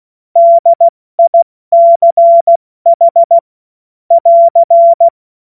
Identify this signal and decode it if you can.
Morse code, heard as a keyed tone, about 16 wpm, spaced as for 14 wpm, 680 Hz. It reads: DICH <AR>
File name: Morsekurs / Witze / Witz 30. Mai 2023 70 F80